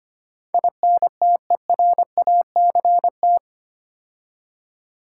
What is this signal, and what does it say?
Morse code, heard as a keyed tone, about 25 wpm, 700 Hz. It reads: INTERACT
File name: 8 Letter Words - First 3 Letters Given - 25wpm